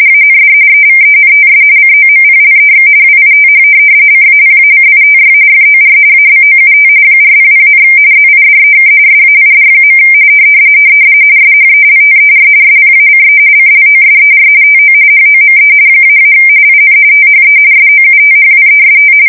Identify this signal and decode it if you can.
RTTY, about 45.45 baud. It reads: AND EVENTS COMPLEX WITH CAPACITY FOR ABOUT 400, ZIDEK SAID. SHELTER SPACE CLOSER TO HOME -- IN THE SOUTHWEST
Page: 170 Shift 2125/2295